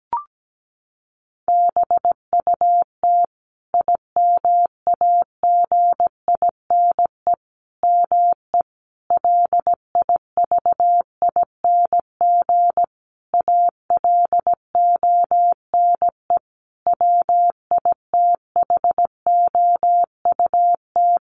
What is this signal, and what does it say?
Morse code, heard as a keyed tone, about 17 wpm, 700 Hz. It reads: BUT IMAGINE ME LIVING ALONE WITHOUT
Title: Sentences from Top 1400 Words 17wpm